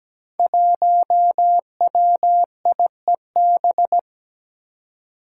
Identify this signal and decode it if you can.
Morse code, heard as a keyed tone, about 17 wpm, 700 Hz. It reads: 1WIEB
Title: Five Challenging Letter-Number Combinations 17wpm